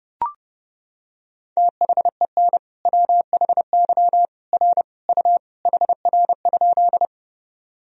Morse code, heard as a keyed tone, about 30 wpm, 700 Hz. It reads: THEN WHY R U HR?
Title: Sentences from Top 200 Words and Abbreviations 30wpm